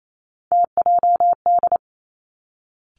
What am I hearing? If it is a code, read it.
Morse code, heard as a keyed tone, about 28 wpm, 700 Hz. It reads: TJD
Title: Three Challenging Letter-Number Combinations 28wpm